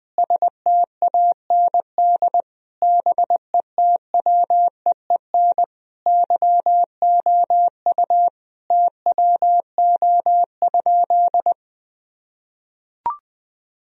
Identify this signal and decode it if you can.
Morse code, heard as a keyed tone, about 20 wpm, 700 Hz. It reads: STAND BETWEEN YOU TWO?
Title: Sentences from Top 1000 Words - Review Part 1 - 20wpm